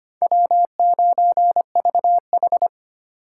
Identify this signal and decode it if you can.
Morse code, heard as a keyed tone, about 33 wpm, 700 Hz. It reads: W9VH